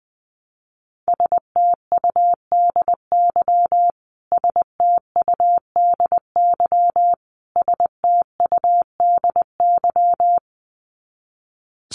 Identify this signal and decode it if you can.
Morse code, heard as a keyed tone, about 20 wpm, 700 Hz. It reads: STUDY STUDY STUDY
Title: Top 300 Words Repeated 20wpm